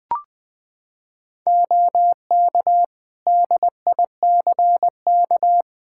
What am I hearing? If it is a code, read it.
Morse code, heard as a keyed tone, about 21 wpm, 700 Hz. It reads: OK DICK
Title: Practice QSO - Exchange 3 - More Info - 20wpm